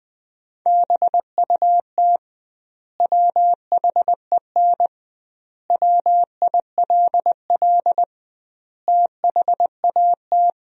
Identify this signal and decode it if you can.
Morse code, heard as a keyed tone, about 20 wpm, 700 Hz. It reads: BUT WHEN WILL THAT
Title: 2x Word-Spacing - Sentences from Top 100 Words - 20wpm